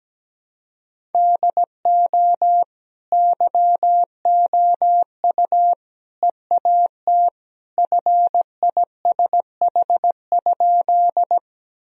Morse code, heard as a keyed tone, about 17 wpm, 700 Hz. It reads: DO YOU EAT FISH?
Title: Sentences from Top 1000 Words - Review Part 1 - 17wpm